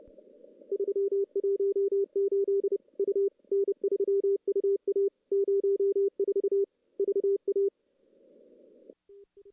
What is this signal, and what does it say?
Morse code, heard as a keyed tone, about 29 wpm, 400 Hz. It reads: 318 U N3UA 04 VA
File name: contests / CW-SS-k5tr- -1600